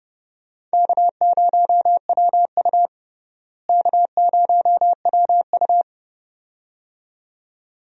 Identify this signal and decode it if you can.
Morse code, heard as a keyed tone, about 30 wpm, 700 Hz. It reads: K0WU K0WU
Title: USA Call Signs - 4 Characters - Repeated 2x - 30wpm